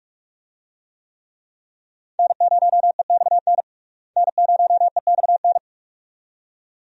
Morse code, heard as a keyed tone, about 42 wpm, 700 Hz. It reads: N0EXN N0EXN
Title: USA Call Signs - 5 Characters - Repeated 2x - 45wpm